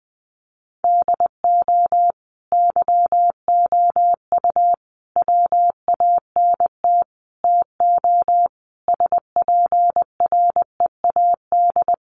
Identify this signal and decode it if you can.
Morse code, heard as a keyed tone, about 20 wpm, 700 Hz. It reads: DO YOU WANT TO SPREAD
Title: Sentences from Top 1200 Words 20wpm